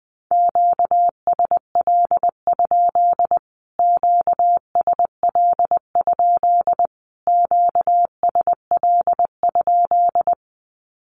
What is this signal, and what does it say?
Morse code, heard as a keyed tone, about 20 wpm, 700 Hz. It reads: QSL? QSL? QSL?
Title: Q-Codes Repeated 3x 20wpm